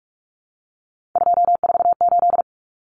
Morse code, heard as a keyed tone, about 44 wpm, 700 Hz. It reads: 248
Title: Numbers - 3-Digit 45wpm